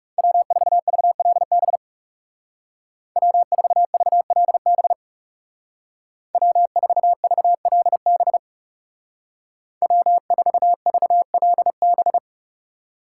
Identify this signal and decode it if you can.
Morse code, heard as a keyed tone, about 37 wpm, 700 Hz. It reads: W4VLB W4VLB W4VLB W4VLB
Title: Speed Racing - USA 1x3 Call Signs 30wpm